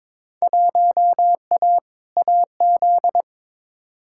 Morse code, heard as a keyed tone, about 22 wpm, 700 Hz. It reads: 1A AZ
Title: Field Day Exchanges - 22wpm